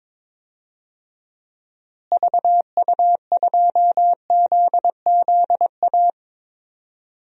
Morse code, heard as a keyed tone, about 22 wpm, 700 Hz. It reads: VU2ZZA
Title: DXCC-100 Call Signs - 6 Characters - Repeated 3x - 22wpm